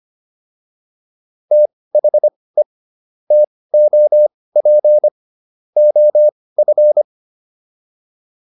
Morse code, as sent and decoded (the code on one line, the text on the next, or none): - .... / - --- .--. / --- ..-.
THE TOP OF